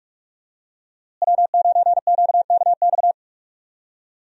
Morse code, heard as a keyed tone, about 41 wpm, 700 Hz. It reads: W9QKX
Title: USA Call Signs - 5 Characters - Repeated 2x - 45wpm